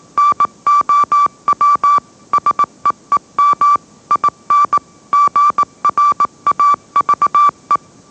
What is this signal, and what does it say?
Morse code, heard as a keyed tone, about 18 wpm, 1160 Hz. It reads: NOW SEEM IN GRAVE
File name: mrx / mrx081 20wpm